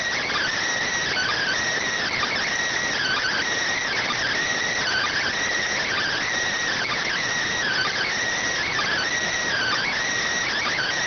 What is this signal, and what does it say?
Emergency Alert System (SAME) tones (header bursts), heard as harsh buzzing bursts. It is heard from 1.1 to 9.9 s.